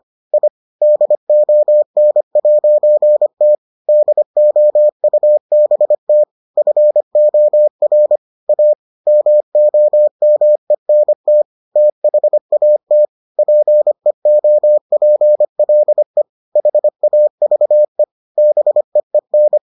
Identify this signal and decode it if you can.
Morse code, heard as a keyed tone, about 25 wpm, 600 Hz. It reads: I DON'T DOUBT FOR A MOMENT THAT PEOPLE HAVE BEEN